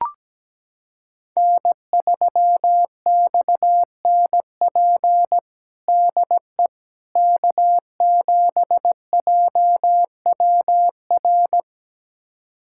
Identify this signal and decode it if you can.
Morse code, heard as a keyed tone, about 17 wpm, 700 Hz. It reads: N3XNP DE K7JWR
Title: USA Callsign DE Callsign - 5 Characters - 17wpm